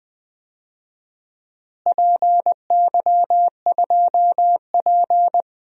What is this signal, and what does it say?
Morse code, heard as a keyed tone, about 20 wpm, 700 Hz. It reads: PY2P
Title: DXCC-100 Call Signs - 4 Characters - Repeated 3x - 20wpm